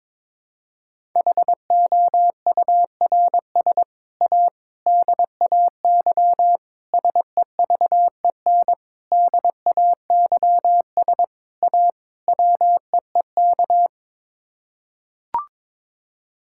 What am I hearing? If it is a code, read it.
Morse code, heard as a keyed tone, about 22 wpm, 700 Hz. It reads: HOURS A DAY SEVEN DAYS A WEEK
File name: practice / 22wpm / Sets of 7 Words - 22wpm